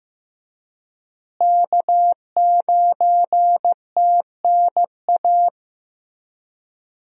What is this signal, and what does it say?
Morse code, heard as a keyed tone, about 14 wpm, 700 Hz. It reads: K9TNA
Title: USA Call Signs - 5 Characters - Repeated 3x - 15wpm